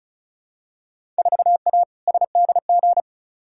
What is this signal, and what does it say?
Morse code, heard as a keyed tone, about 36 wpm, 700 Hz. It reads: 4A SDG
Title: Field Day Exchanges - 35wpm